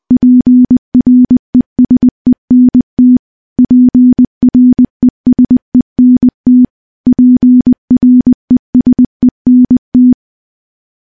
Morse code, heard as a keyed tone, about 20 wpm, 250 Hz. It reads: PRESENT PRESENT PRESENT
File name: morse-training-20WPM